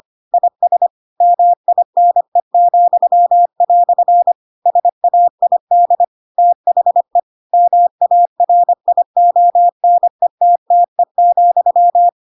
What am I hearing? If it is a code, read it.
Morse code, heard as a keyed tone, about 25 wpm, 700 Hz. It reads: IS MINE," SAID THE MARIONETTE,
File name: Pinocchio / Pinocchio ch4 25 25 ik1hge